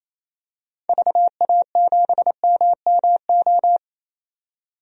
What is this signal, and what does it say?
Morse code, heard as a keyed tone, about 28 wpm, 700 Hz. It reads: VA7MMO
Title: DXCC-100 Call Signs - 6 Characters - 28wpm